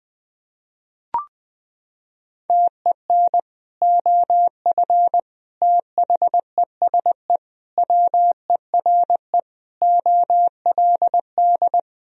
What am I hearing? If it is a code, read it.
Morse code, heard as a keyed tone, about 20 wpm, 700 Hz. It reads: TEN OF THESE WERE OLD
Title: Sentences from Top 1000 Words - Review Part 2 - 20wpm